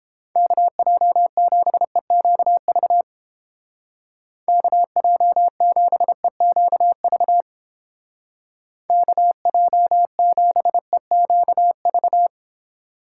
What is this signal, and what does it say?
Morse code, heard as a keyed tone, about 30 wpm, 700 Hz. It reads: KJ7EQV KJ7EQV KJ7EQV